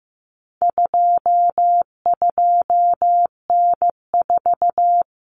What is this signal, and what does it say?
Morse code, heard as a keyed tone, about 15 wpm, 700 Hz. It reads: 22N4
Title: Common Cut Numbers - 4-Digit 15wpm